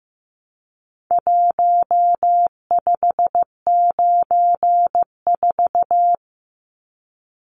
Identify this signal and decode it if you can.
Morse code, heard as a keyed tone, about 15 wpm, 700 Hz. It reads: 1594
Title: Numbers - Contest Style - 4-Digit 15wpm